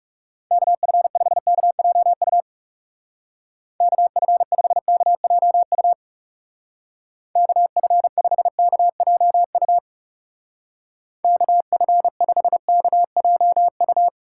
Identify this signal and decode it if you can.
Morse code, heard as a keyed tone, about 38 wpm, 700 Hz. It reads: KF5KJU KF5KJU KF5KJU KF5KJU